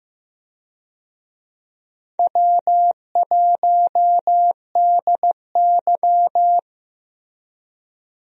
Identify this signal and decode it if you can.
Morse code, heard as a keyed tone, about 15 wpm, 700 Hz. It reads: W1DY
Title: USA Call Signs - 4 Characters - 15wpm